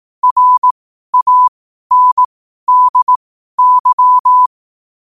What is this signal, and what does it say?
Morse code, heard as a keyed tone, about 18 wpm, spaced as for 12 wpm, 1000 Hz. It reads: RANDY
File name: qotdfiles / 12wpm